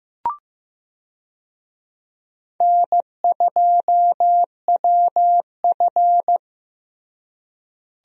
Morse code, heard as a keyed tone, about 15 wpm, 700 Hz. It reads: N2WF
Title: USA Call Signs - 4 Characters - Repeated 2x - 15wpm